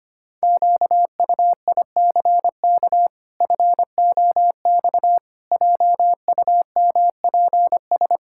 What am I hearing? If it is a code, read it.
Morse code, heard as a keyed tone, about 25 wpm, 700 Hz. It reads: QUICK FOX JUMPS
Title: Pangram Challenge v2 - 25wpm